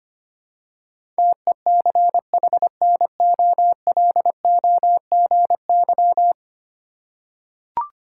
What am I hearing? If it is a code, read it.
Morse code, heard as a keyed tone, about 25 wpm, 700 Hz. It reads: TECHNOLOGY
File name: 10 Letter Words - First 3 Letters Given - 25wpm